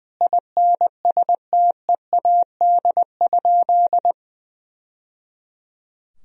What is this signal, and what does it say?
Morse code, heard as a keyed tone, about 21 wpm, 700 Hz. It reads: INSTEAD?
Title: Sentences from Top 1000 Words - Review Part 4 - 20wpm